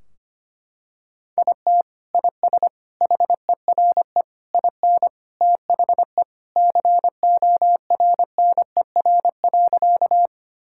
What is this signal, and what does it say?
Morse code, heard as a keyed tone, about 25 wpm, 700 Hz. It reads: IT IS HERE IN THE CORNER.
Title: Sentences from Top 1600 Words 25wpm